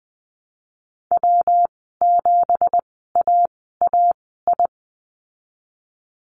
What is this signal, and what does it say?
Morse code, heard as a keyed tone, about 20 wpm, spaced as for 15 wpm, 700 Hz. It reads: W7AAI
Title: 173 Lesson 28g - Number 9 - Call Signs 20wpm